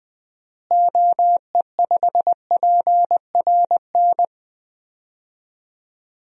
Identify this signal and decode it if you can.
Morse code, heard as a keyed tone, about 20 wpm, 700 Hz. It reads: OE5PRN
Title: DXCC-100 Call Signs - 6 Characters - Repeated 3x - 20wpm